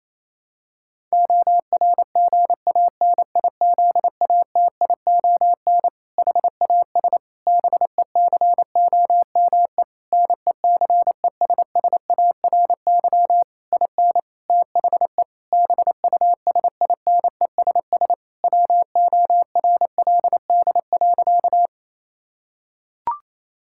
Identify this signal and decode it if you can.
Morse code, heard as a keyed tone, about 28 wpm, 700 Hz. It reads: ORGANIZATION HAS BECOME NECESSARY IN THE BUSINESS WORLD.
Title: Sentences from Top 1000 Words - Review Part 5 - 28wpm